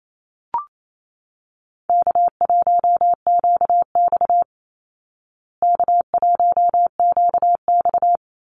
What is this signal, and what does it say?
Morse code, heard as a keyed tone, about 28 wpm, 700 Hz. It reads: K1QX K1QX
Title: Call Signs Repeated 2x - 28wpm